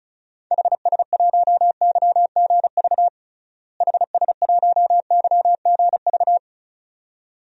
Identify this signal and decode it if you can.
Morse code, heard as a keyed tone, about 34 wpm, 700 Hz. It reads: HS1YGV HS1YGV
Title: DXCC-100 Call Signs - 6 Characters - Repeated 2x - 35wpm